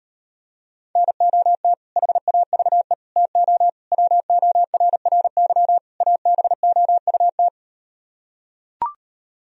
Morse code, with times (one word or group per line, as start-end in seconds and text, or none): 1.0-1.7 NOT
2.0-2.9 HAVE
3.2-3.7 TO
3.9-5.8 WORRY
6.0-7.5 ABOUT